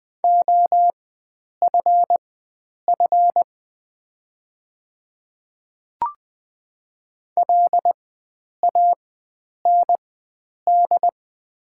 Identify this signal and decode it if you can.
Morse code, heard as a keyed tone, about 20 wpm, spaced as for 9 wpm, 700 Hz. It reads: OFF LAND
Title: 4x Character Spacing - Top 300 Words - 20wpm